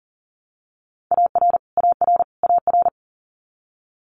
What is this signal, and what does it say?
Morse code, heard as a keyed tone, about 41 wpm, 700 Hz. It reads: AR AR AR
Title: Multi-letter Phonemes Repeated 3x 40wpm